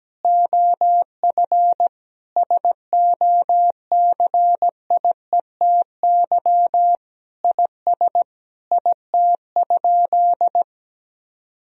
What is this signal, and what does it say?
Morse code, heard as a keyed tone, about 17 wpm, 700 Hz. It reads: OF SOCIETY IS IT?
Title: Sentences from Top 1000 Words - Review Part 4 - 17wpm